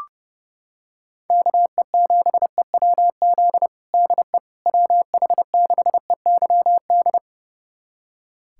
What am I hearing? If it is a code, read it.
Morse code, heard as a keyed tone, about 32 wpm, 700 Hz. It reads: KE7EWZ DE WH6EYD
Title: USA Callsign DE Callsign 30wpm